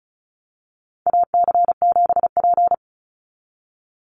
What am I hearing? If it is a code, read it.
Morse code, heard as a keyed tone, about 34 wpm, 700 Hz. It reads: AC7P
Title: NA SOTA Chaser Callsigns 35wpm